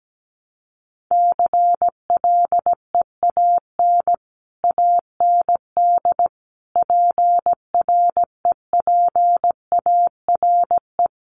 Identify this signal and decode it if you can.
Morse code, heard as a keyed tone, about 17 wpm, 700 Hz. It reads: CLEAN AND PREPARE
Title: Sentences from Top 1100 Words 17wpm